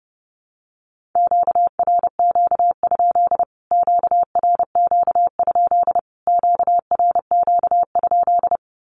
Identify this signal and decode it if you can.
Morse code, heard as a keyed tone, about 30 wpm, 700 Hz. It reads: QRQ? QRQ? QRQ?
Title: Q-Codes Repeated 3x 30wpm